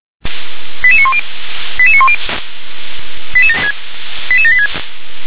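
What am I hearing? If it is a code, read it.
Five-tone selective call (ZVEI), heard as quick runs of five tones, tones of about 70 ms. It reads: 88910 88910 88786
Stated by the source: Five-tone ZVEI selective call received by SDR.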